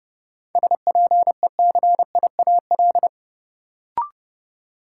Morse code, heard as a keyed tone, about 30 wpm, 700 Hz. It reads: SPECIAL
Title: Sentences from Top 1200 Words - New Word Drill Alt - 30wpm